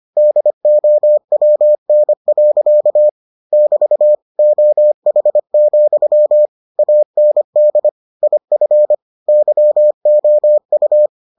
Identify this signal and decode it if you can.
Morse code, heard as a keyed tone, about 25 wpm, 600 Hz. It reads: DOWN. <BT> OH, AND IF YOU